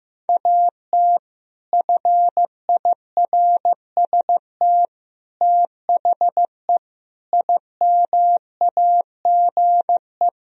Morse code, heard as a keyed tone, about 15 wpm, 700 Hz. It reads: AT FIRST THE IMAGE
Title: Sentences from Top 1000 Words - Review Part 4 - 15wpm